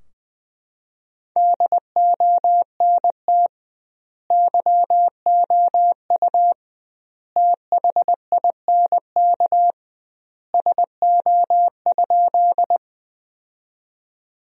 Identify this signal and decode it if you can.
Morse code, heard as a keyed tone, about 20 wpm, 700 Hz. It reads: DONT YOU THINK SO?
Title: 2x Word-Spacing - Sentences from Top 300 Words - 20wpm